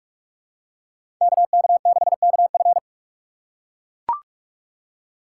Morse code, heard as a keyed tone, about 42 wpm, 700 Hz. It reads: KK6KF
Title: USA Call Signs - 5 Characters - 6x Character Spacing - 45wpm